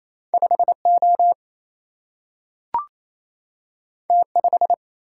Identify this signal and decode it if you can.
Morse code, heard as a keyed tone, about 28 wpm, 700 Hz. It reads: 5 O T 5
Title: Single-Letter Single-Number 28wpm